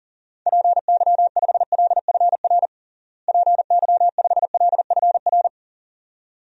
Morse code, heard as a keyed tone, about 40 wpm, 700 Hz. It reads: PY5LFR PY5LFR
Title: DXCC-100 Call Signs - 6 Characters - Repeated 2x - 40wpm